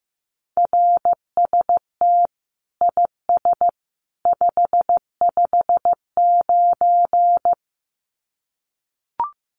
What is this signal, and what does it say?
Morse code, heard as a keyed tone, about 15 wpm, 700 Hz. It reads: RST IS 559
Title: Common QSO Phrases 15wpm